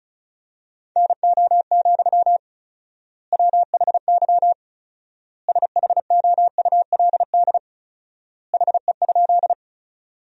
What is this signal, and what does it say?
Morse code, heard as a keyed tone, about 36 wpm, 700 Hz. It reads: NO, WHY SHOULD HE?
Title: 4x Word-Spacing - Sentences from Top 300 Words - 35wpm